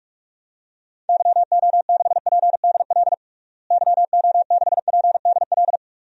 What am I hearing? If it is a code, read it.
Morse code, heard as a keyed tone, about 45 wpm, 700 Hz. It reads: YO6PDL YO6PDL
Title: DXCC-100 Call Signs - 6 Characters - Repeated 2x - 45wpm